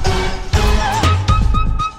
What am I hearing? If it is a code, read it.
There is no code.